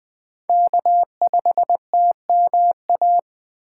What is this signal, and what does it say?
Morse code, heard as a keyed tone, about 20 wpm, 700 Hz. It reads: K5TMA
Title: USA Call Signs - 5 Characters - 8x Character Spacing - 20wpm